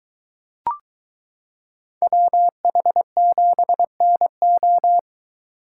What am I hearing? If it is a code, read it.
Morse code, heard as a keyed tone, about 22 wpm, 700 Hz. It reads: WH7NO